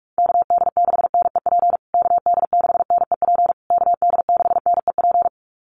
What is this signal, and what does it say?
Morse code, heard as a keyed tone, about 44 wpm, 690 Hz. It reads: KD6NEP KD6NEP KD6NEP
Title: USA Call Signs Repeated 3x 45wpm